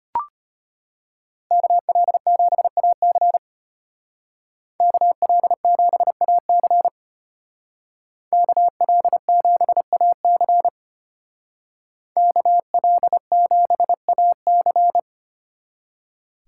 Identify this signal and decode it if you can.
Morse code, heard as a keyed tone, about 32 wpm, 700 Hz. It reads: KL7AC KL7AC KL7AC KL7AC